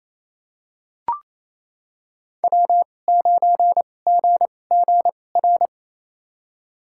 Morse code, heard as a keyed tone, about 28 wpm, spaced as for 20 wpm, 700 Hz. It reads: W9GGR
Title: USA Call Signs - 5 Characters - 2x Character Spacing - 28wpm